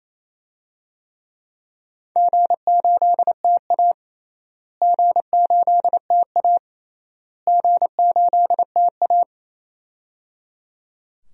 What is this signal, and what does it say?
Morse code, heard as a keyed tone, about 28 wpm, 700 Hz. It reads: G8TA G8TA G8TA
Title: DXCC-100 Call Signs - 4 Characters - Repeated 3x - 28wpm